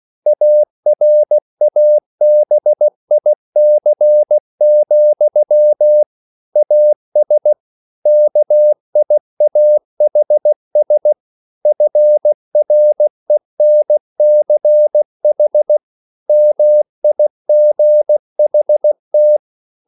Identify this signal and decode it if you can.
Morse code, heard as a keyed tone, about 16 wpm, 600 Hz. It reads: ARABIC, AS KIAHS FRENCH MIGHT